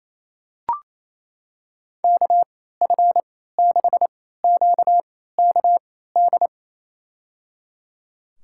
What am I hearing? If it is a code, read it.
Morse code, heard as a keyed tone, about 28 wpm, spaced as for 16 wpm, 700 Hz. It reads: KF6QKD
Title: USA Call Signs - 6 Characters - 3x Character Spacing - 28wpm